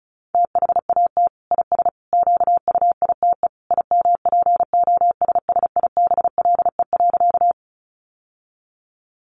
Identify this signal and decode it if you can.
Morse code, heard as a keyed tone, about 34 wpm, 700 Hz. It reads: THAT IS QUITE IMPOSSIBLE.
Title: Sentences from Top 1100 Words 35wpm